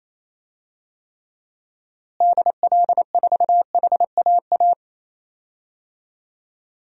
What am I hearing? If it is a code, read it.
Morse code, heard as a keyed tone, about 27 wpm, 700 Hz. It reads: DL4HAA
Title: DXCC-100 Call Signs - 6 Characters - Repeated 3x - 28wpm